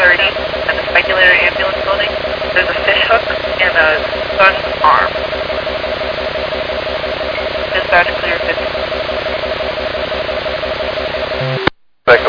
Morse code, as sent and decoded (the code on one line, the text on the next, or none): none